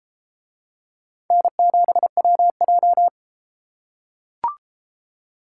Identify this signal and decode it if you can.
Morse code, heard as a keyed tone, about 30 wpm, 700 Hz. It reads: N7WJ